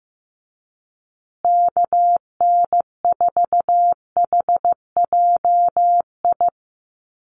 Morse code, heard as a keyed tone, about 15 wpm, 700 Hz. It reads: KN4HJI
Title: USA Call Signs Repeated 2x 15wpm